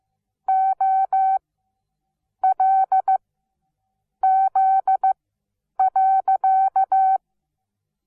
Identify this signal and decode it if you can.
Morse code, heard as a keyed tone, about 15 wpm, spaced as for 7 wpm, 760 Hz. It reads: OLZ.
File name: Onlinelehrgang / morsen / DJ4UF Morsen17